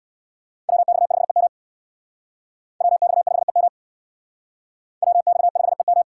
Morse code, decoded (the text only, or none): WX5EM WX5EM WX5EM